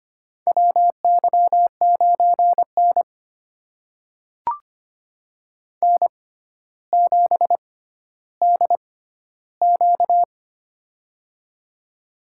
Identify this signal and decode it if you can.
Morse code, heard as a keyed tone, about 25 wpm, spaced as for 13 wpm, 700 Hz. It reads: WY9N N7DQ
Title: USA Call Signs - 4 Characters - 6x Character Spacing - 25wpm